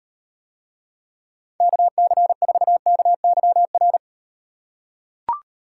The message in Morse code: KC4KYR

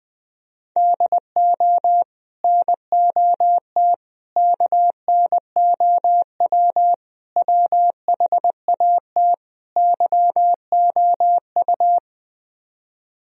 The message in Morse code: DO NOT KNOW WHAT YOU

